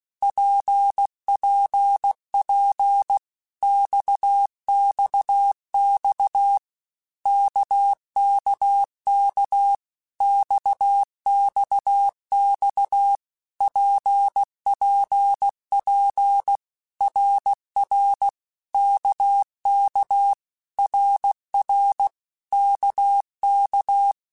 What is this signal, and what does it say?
Morse code, heard as a keyed tone, about 16 wpm, 800 Hz. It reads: PPP XXX KKK XXX PPP RR KK RR KK
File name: Basic Letter Groups By7Groups / RKPX / RKPX@15